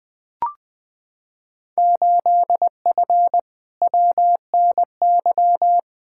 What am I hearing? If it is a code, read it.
Morse code, heard as a keyed tone, about 20 wpm, 700 Hz. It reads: 8F WNY
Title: Field Day Exchanges Encore - 20wpm